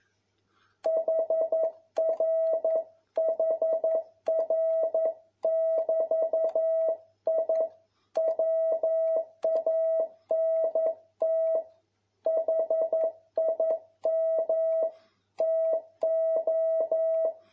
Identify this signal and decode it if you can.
Morse code, heard as a keyed tone, about 11 wpm, 650 Hz. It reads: HRHR<BT>I WANT HIM TO
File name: jh8cbh / 2021cw